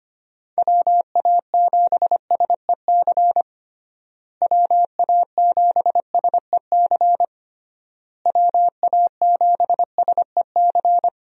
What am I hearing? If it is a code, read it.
Morse code, heard as a keyed tone, about 25 wpm, 700 Hz. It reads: WA7SEC WA7SEC WA7SEC
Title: USA Call Signs - 6 Characters - Repeated 3x - 25wpm